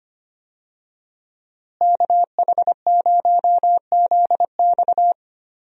Morse code, heard as a keyed tone, about 25 wpm, 700 Hz. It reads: KH0ZX
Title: USA Call Signs - 5 Characters - Repeated 3x - 25wpm